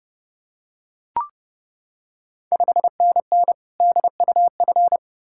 Morse code, heard as a keyed tone, about 30 wpm, 700 Hz. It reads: 5NN DUF
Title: Ontario QSO Party Exchange 30wpm